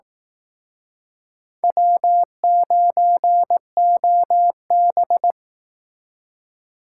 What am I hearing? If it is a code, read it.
Morse code, heard as a keyed tone, about 18 wpm, 700 Hz. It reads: W9OB
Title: Speed Racing - USA 1x2 Call Signs 15wpm